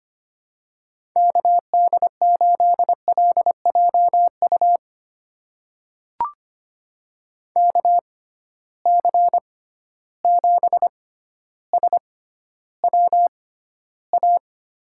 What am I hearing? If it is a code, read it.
Morse code, heard as a keyed tone, about 25 wpm, spaced as for 13 wpm, 700 Hz. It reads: KD8LJU KC7SWA
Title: USA Call Signs - 6x Character Spacing - 25wpm